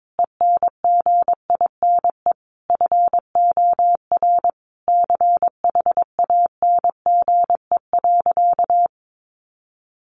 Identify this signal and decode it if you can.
Morse code, heard as a keyed tone, about 22 wpm, 700 Hz. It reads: ENGINE FOR CHANGE.
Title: Sentences from Top 1500 Words 22wpm